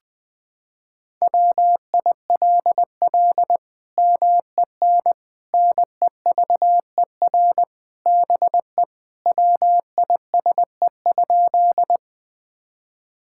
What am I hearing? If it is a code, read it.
Morse code, heard as a keyed tone, about 21 wpm, 700 Hz. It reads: WILL MEN NEVER BE WISE?
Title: Sentences from Top 1600 Words 20wpm